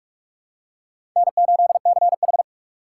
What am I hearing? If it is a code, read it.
Morse code, heard as a keyed tone, about 46 wpm, 700 Hz. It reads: N8CH
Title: USA Call Signs - 4 Characters - 2x Character Spacing - 45wpm